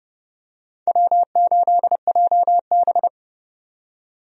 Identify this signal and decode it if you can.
Morse code, heard as a keyed tone, about 30 wpm, 700 Hz. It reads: W8JB
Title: USA Call Signs - 4 Characters - 30wpm